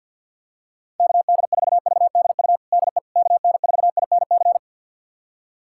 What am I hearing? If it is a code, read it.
Morse code, heard as a keyed tone, about 51 wpm, 700 Hz. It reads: KD4VDU DE KN4INC
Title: USA Callsign DE Callsign - 6 Characters - 50wpm